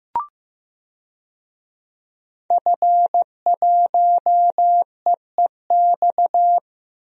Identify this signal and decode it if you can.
Morse code, heard as a keyed tone, about 15 wpm, 700 Hz. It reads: F1EEX
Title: DXCC-100 Call Signs - 5 Characters - Repeated 2x - 15wpm